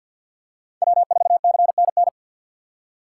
Morse code, heard as a keyed tone, about 47 wpm, 690 Hz. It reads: W4CNN
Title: USA Call Signs - 5 Characters - 6x Character Spacing - 50wpm